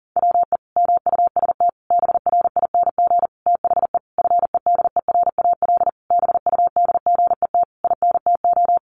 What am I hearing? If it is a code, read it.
Morse code, heard as a keyed tone, about 40 wpm, 700 Hz. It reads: WE MUST BRING THE FEDERAL BUDGET INTO
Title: Sentences from Top 1200 Words 40wpm